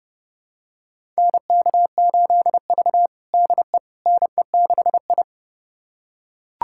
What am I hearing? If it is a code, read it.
Morse code, heard as a keyed tone, about 30 wpm, 700 Hz. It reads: NK8V DE NE6I
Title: USA Callsign DE Callsign - 4 Characters - 30wpm